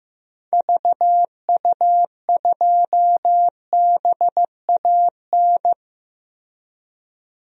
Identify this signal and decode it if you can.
Morse code, heard as a keyed tone, about 15 wpm, 700 Hz. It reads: VU2BAN
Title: DXCC-100 Call Signs - 6 Characters - Repeated 3x - 15wpm